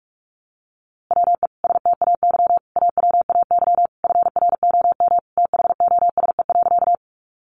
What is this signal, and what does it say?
Morse code, heard as a keyed tone, about 44 wpm, 700 Hz. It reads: WE STAY AWAY FROM THOSE.
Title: A Sound Of Thunder 45wpm